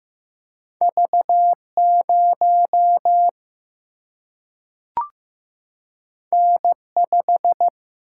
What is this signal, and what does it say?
Morse code, heard as a keyed tone, about 15 wpm, 700 Hz. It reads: V 0 N 5
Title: Single-Letter Single-Number 15wpm